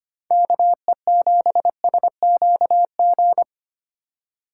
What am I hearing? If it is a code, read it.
Morse code, heard as a keyed tone, about 25 wpm, 700 Hz. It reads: KE7SQG
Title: USA Call Signs - 6 Characters - 3x Character Spacing - 25wpm